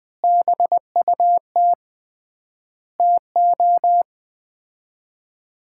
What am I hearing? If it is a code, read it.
Morse code, heard as a keyed tone, about 20 wpm, 690 Hz. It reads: BUT TO
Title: 3x Word-Spacing - Sets of 5 Words - 20wpm